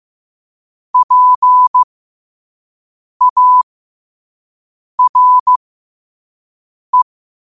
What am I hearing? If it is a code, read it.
Morse code, heard as a keyed tone, about 15 wpm, spaced as for 5 wpm, 1000 Hz. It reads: PARE
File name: qotdfiles / 5wpm